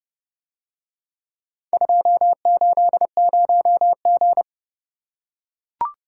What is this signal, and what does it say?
Morse code, heard as a keyed tone, about 29 wpm, 700 Hz. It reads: 280G